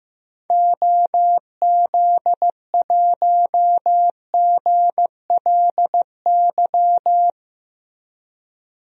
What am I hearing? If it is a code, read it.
Morse code, heard as a keyed tone, about 15 wpm, 700 Hz. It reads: OZ1GLY
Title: DXCC-100 Call Signs - 6 Characters - Repeated 2x - 15wpm